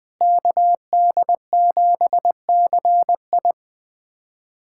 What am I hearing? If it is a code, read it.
Morse code, heard as a keyed tone, about 20 wpm, 700 Hz. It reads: KD7CI
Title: USA Callsign DE Callsign - 5 Characters - 20wpm